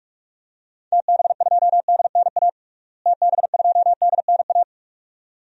Morse code, heard as a keyed tone, about 46 wpm, 700 Hz. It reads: TB2DNA TB2DNA
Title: DXCC-100 Call Signs - 6 Characters - Repeated 2x - 45wpm